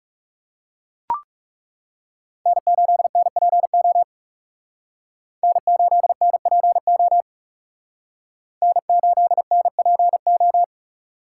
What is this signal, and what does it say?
Morse code, heard as a keyed tone, about 38 wpm, 700 Hz. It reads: N8NPO N8NPO N8NPO